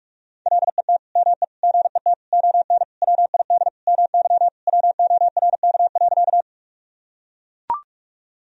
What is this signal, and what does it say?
Morse code, heard as a keyed tone, about 41 wpm, 700 Hz. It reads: LET ME GET ON WID MY WORK.
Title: Sentences from Top 300 Words and Abbreviations 45wpm